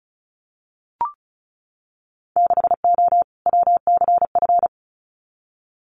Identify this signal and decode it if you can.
Morse code, heard as a keyed tone, about 36 wpm, 700 Hz. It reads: 6O WCF
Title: Winter Field Day Exchanges Encore 35wpm